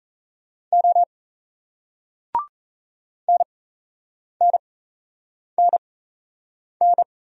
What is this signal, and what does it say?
Morse code, heard as a keyed tone, about 38 wpm, spaced as for 8 wpm, 700 Hz. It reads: O NNNN